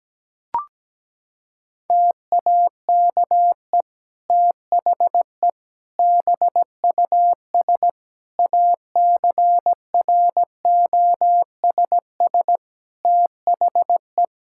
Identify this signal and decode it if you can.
Morse code, heard as a keyed tone, about 17 wpm, 700 Hz. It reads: TAKE THE BUS ACROSS THE
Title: Sentences from Top 1000 Words - Review Part 6 - 17wpm